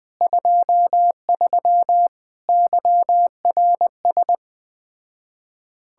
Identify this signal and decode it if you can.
Morse code, heard as a keyed tone, about 20 wpm, 700 Hz. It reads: 23 YRS
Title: Speed-Racing Common QSO Phrases - 17wpm